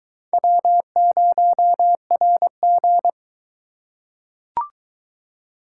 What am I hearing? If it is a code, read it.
Morse code, heard as a keyed tone, about 22 wpm, 700 Hz. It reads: W0RG